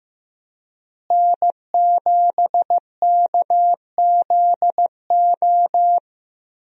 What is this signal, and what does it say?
Morse code, heard as a keyed tone, about 15 wpm, 700 Hz. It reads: N7KZO
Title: USA Call Signs - 5 Characters - Repeated 3x - 15wpm